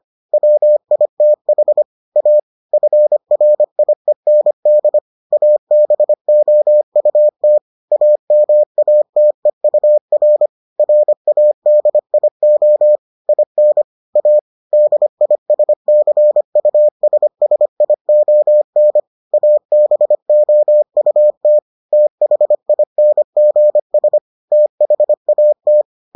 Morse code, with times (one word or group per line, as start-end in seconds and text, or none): 0.3-1.8 WITH
2.2-2.4 A
2.7-5.0 FRIEND
5.3-7.6 ABOUT
7.9-10.5 AMATEUR
10.8-13.0 RADIO
13.3-13.8 IN
14.2-14.4 A
14.7-19.0 DISCUSSION
19.3-21.6 ABOUT
21.9-24.2 THINGS
24.5-25.8 THAT